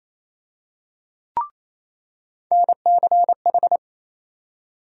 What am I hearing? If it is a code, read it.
Morse code, heard as a keyed tone, about 28 wpm, 700 Hz. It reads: NCH
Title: USA Call Sign Postfixes - 3 Letters 28wpm